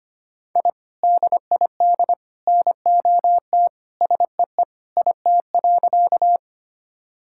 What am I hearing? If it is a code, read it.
Morse code, heard as a keyed tone, about 25 wpm, 700 Hz. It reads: I DID NOT SEE IT.
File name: practice / 25wpm / Sentences with 3-Word Phrases - 25wpm